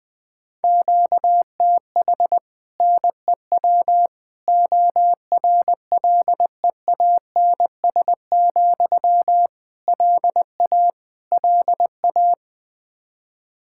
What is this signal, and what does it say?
Morse code, heard as a keyed tone, about 21 wpm, 700 Hz. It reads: QTH NEW ORLEANS, LA LA
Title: Practice QSO - Exchange 1 - Beginning - 20wpm